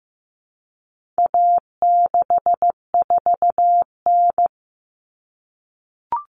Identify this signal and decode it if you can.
Morse code, heard as a keyed tone, about 15 wpm, 700 Hz. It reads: A64N
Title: Common Cut Numbers - 4-Digit 15wpm